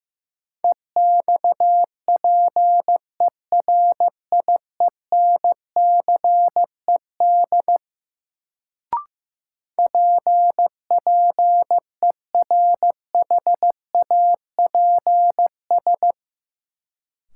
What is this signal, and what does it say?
Morse code, heard as a keyed tone, about 15 wpm, 700 Hz. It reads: EXPERIENCED PPERHAPS
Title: Sentences from Top 1100 Words - New Word Drill - 15wpm